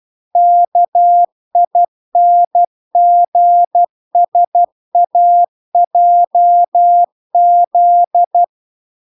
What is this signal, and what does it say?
Morse code, heard as a keyed tone, about 12 wpm, 700 Hz. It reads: KINGSAJZ